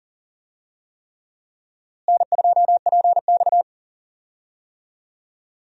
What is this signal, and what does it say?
Morse code, heard as a keyed tone, about 37 wpm, 700 Hz. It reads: N2PX
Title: USA Call Signs - 4 Characters - Repeated 3x - 40wpm